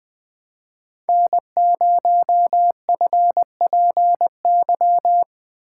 Morse code, heard as a keyed tone, about 20 wpm, 700 Hz. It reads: N0FPY
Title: USA Call Signs - 2x Character Spacing - 20wpm